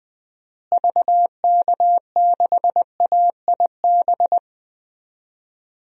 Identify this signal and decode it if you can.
Morse code, heard as a keyed tone, about 20 wpm, 700 Hz. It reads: VK6AIB
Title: DXCC-100 Call Signs - 6 Characters - Repeated 2x - 20wpm